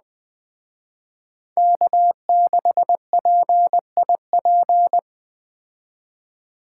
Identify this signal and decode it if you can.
Morse code, heard as a keyed tone, about 21 wpm, 700 Hz. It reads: K6PIP